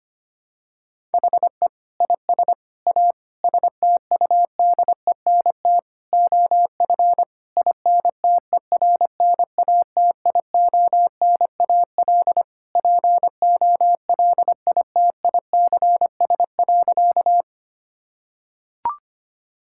Morse code, heard as a keyed tone, about 25 wpm, 700 Hz. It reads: HE IS A STUDENT OF INTERNATIONAL POLITICS.
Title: Sentences from Top 1000 Words - Review Part 6 - 25wpm